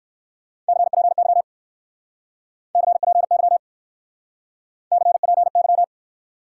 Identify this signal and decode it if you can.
Morse code, heard as a keyed tone, about 55 wpm, 700 Hz. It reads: CPY CPY CPY
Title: Speed-Racing Top 180 QSO Elements 45wpm